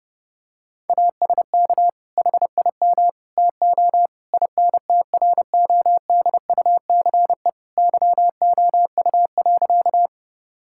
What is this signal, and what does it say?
Morse code, heard as a keyed tone, about 32 wpm, 700 Hz. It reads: ASK HIM TO INTRODUCE YOU.
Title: Sentences from Top 1600 Words 30wpm